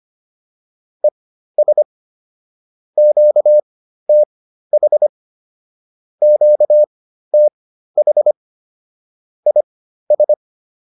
Morse code, heard as a keyed tone, about 25 wpm, spaced as for 13 wpm, 600 Hz. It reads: ES QTH QTH IS